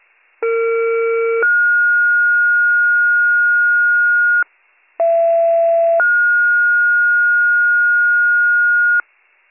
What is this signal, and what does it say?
Two-tone page, heard as a one-second tone, then a three-second tone, about 460 then 1430 Hz, about 670 then 1430 Hz. It is heard from 0.4 to 9.0 s.